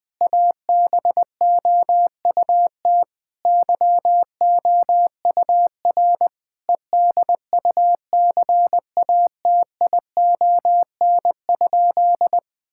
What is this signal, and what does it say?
Morse code, heard as a keyed tone, about 21 wpm, 700 Hz. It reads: ABOUT YOUR EDUCATION?
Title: Sentences from Top 1000 Words - Review Part 6 - 20wpm